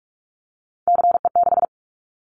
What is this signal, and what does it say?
Morse code, heard as a keyed tone, about 46 wpm, 700 Hz. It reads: CE6
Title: Callsign Prefixes for Top 100 DXCC - 45wpm